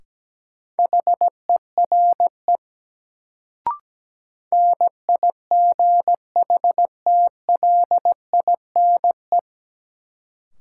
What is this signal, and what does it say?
Morse code, heard as a keyed tone, about 17 wpm, 700 Hz. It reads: HERE NIGHTLINE
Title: Sets of 5 Words - New Word Drill - 17wpm